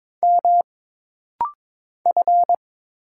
Morse code, heard as a keyed tone, about 22 wpm, 700 Hz. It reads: M F
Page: This rapid-fire timing is geared to push your ICR (Instant Character Recognition) skills.